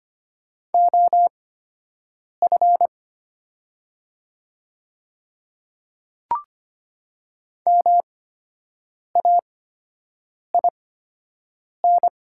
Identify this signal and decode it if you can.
Morse code, heard as a keyed tone, about 25 wpm, spaced as for 7 wpm, 700 Hz. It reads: OF MAIN
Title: 8x Character Spacing - Top 300 Words - 25wpm